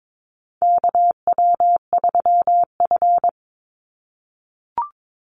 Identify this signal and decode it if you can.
Morse code, heard as a keyed tone, about 22 wpm, 700 Hz. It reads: KW3F
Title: Call Signs - 22wpm